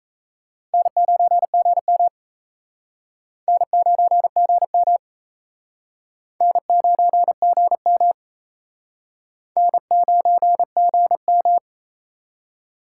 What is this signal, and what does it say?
Morse code, heard as a keyed tone, about 36 wpm, 700 Hz. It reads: N9GM N9GM N9GM N9GM